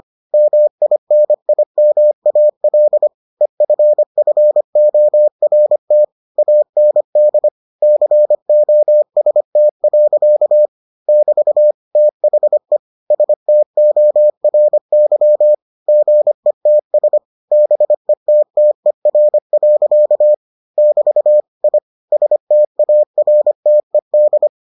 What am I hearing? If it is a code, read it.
Morse code, heard as a keyed tone, about 25 wpm, 600 Hz. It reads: MINIMAL EFFORT AND COST. <BT> THE STORY GETS BETTER. <BT> I STARTED